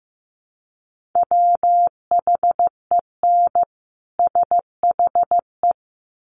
Morse code, heard as a keyed tone, about 15 wpm, 700 Hz. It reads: WHEN SHE
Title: Sets of 2 Words 15wpm